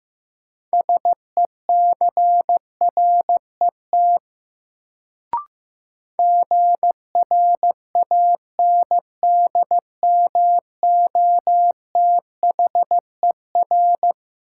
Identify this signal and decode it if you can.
Morse code, heard as a keyed tone, about 15 wpm, 700 Hz. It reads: SECRET GRANDMOTHER
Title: Sentences from Top 1200 Words - New Word Drill Alt - 15wpm